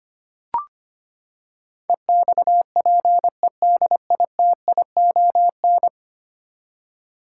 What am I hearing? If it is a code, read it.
Morse code, heard as a keyed tone, about 25 wpm, 700 Hz. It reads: EXPEDITION
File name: Ten Letter Words - 25wpm